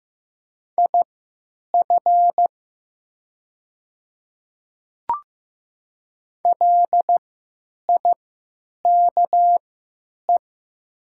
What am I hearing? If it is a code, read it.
Morse code, heard as a keyed tone, about 15 wpm, spaced as for 9 wpm, 700 Hz. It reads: IF LIKE
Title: 3x Character Spacing - Top 200 Words - 15wpm